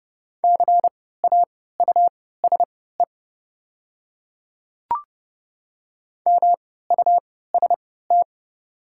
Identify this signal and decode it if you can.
Morse code, heard as a keyed tone, about 30 wpm, spaced as for 17 wpm, 690 Hz. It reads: CAUSE MUST
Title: 3x Character Spacing - Top 200 Words - 30wpm